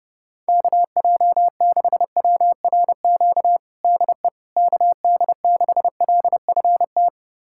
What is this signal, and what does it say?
Morse code, heard as a keyed tone, about 30 wpm, 700 Hz. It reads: KJ6WRQ DE KD6LFT
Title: USA Callsign DE Callsign - 6 Characters - 30wpm